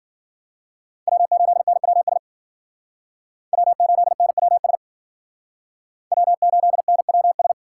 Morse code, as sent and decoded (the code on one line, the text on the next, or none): .-- ---.. -. .-- ... / .-- ---.. -. .-- ... / .-- ---.. -. .-- ...
W8NWS W8NWS W8NWS